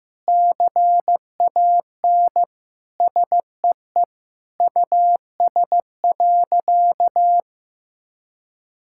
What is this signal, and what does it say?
Morse code, heard as a keyed tone, about 15 wpm, 700 Hz. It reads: CAN SEE US.
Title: Sentences from Top 1600 Words 15wpm